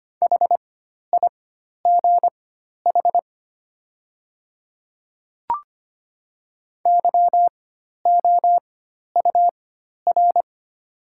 Morse code, heard as a keyed tone, about 25 wpm, spaced as for 12 wpm, 700 Hz. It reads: HIGH YOUR
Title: 4x Character Spacing - Top 300 Words - 25wpm